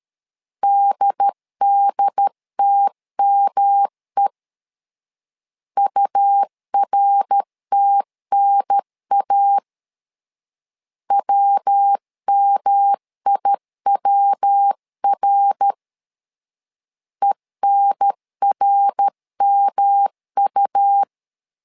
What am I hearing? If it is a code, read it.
Morse code, heard as a keyed tone, about 13 wpm, 800 Hz. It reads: DDTME URTNA WMIWR ENRMU
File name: LEZIONI / 3 MWRO / 3 MWR 12wpm 1711